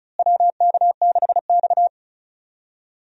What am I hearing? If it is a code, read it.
Morse code, heard as a keyed tone, about 36 wpm, 700 Hz. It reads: WK6X
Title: USA Call Signs - 4 Characters - Repeated 2x - 35wpm